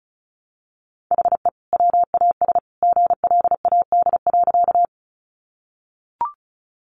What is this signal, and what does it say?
Morse code, heard as a keyed tone, about 34 wpm, 700 Hz. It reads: HE WAS GLAD.
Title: Sentences from Top 800 Words 35wpm